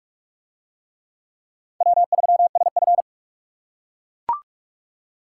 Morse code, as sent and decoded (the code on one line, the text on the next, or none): .-- ...-- ... ..-.
W3SF